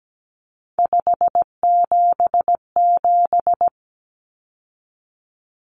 Morse code, heard as a keyed tone, about 17 wpm, 700 Hz. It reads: 577
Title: Numbers - 3-Digit 17wpm